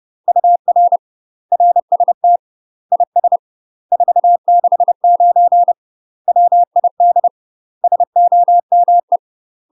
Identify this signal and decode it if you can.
Morse code, heard as a keyed tone, about 30 wpm, 700 Hz. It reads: UR RST IS 469 WID SOME
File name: UK MASTER QSOs TRAINING / 30WPM / UK MASTER QSOs 0016